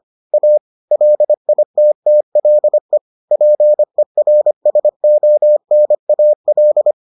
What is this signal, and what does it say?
Morse code, heard as a keyed tone, about 25 wpm, 600 Hz. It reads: A LITTLE PERSONAL